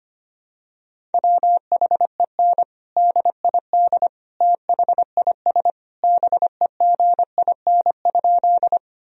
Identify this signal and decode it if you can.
Morse code, heard as a keyed tone, about 25 wpm, 700 Hz. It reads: WHEN DID THIS BEGIN?
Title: Sentences from Top 1000 Words - Review Part 1 - 25wpm